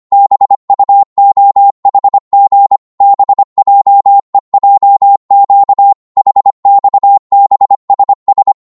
Morse code, heard as a keyed tone, about 25 wpm, 800 Hz. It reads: BUOHG BJEJQ HXBSS